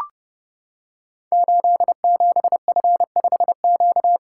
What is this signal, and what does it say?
Morse code, heard as a keyed tone, about 29 wpm, 700 Hz. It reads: 87F5Q
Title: Code Groups - 5 Letter:Numbers - 30wpm